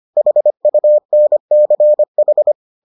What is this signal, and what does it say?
Morse code, heard as a keyed tone, about 25 wpm, 600 Hz. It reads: HUNCH